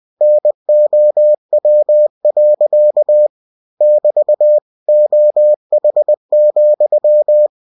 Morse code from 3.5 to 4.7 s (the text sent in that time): <BT>